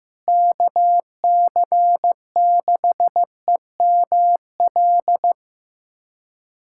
Morse code, heard as a keyed tone, about 16 wpm, 700 Hz. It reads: KC6EML
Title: USA Callsign DE Callsign 15wpm